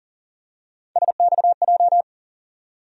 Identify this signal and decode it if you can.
Morse code, heard as a keyed tone, about 40 wpm, 700 Hz. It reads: SXJ
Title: USA Call Sign Postfixes - 3 Letters 40wpm